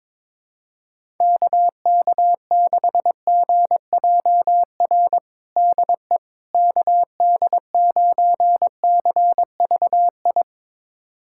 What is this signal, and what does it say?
Morse code, heard as a keyed tone, about 22 wpm, 700 Hz. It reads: KK6GJR DE KD9CVI
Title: USA Callsign DE Callsign - 6 Characters - 22wpm